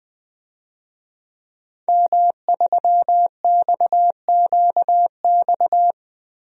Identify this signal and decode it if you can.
Morse code, heard as a keyed tone, about 21 wpm, 700 Hz. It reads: M3XQX
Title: DXCC-100 Call Signs - 5 Characters - Repeated 3x - 20wpm